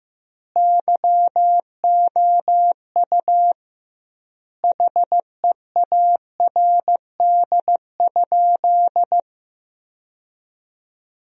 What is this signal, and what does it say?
Morse code, heard as a keyed tone, about 15 wpm, 700 Hz. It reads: YOU HEARD?
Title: 2x Word-Spacing - Sentences from Top 400 Words - 15wpm